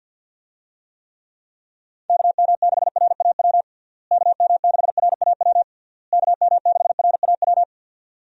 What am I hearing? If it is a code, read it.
Morse code, heard as a keyed tone, about 50 wpm, 700 Hz. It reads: KM6RAW KM6RAW KM6RAW
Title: USA Call Signs - 6 Characters - Repeated 3x - 50wpm